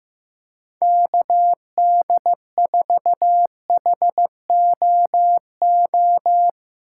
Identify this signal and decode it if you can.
Morse code, heard as a keyed tone, about 15 wpm, 700 Hz. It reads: KD4HOO
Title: USA Call Signs - 2x Character Spacing - 15wpm